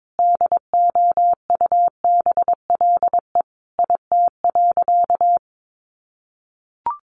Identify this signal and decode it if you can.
Morse code, heard as a keyed tone, about 22 wpm, 700 Hz. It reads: DOUBLE IT.
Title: Sentences from Top 1400 Words 22wpm